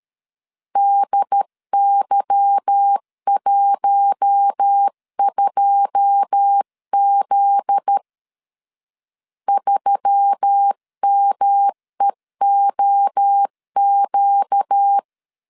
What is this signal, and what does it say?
Morse code, heard as a keyed tone, about 13 wpm, 800 Hz. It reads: DY12Z 3MEOQ
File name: LEZIONI / 9 28 12wpm 2351